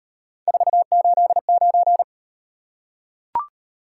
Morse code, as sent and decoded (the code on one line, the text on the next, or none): ....- ---.. ----.
489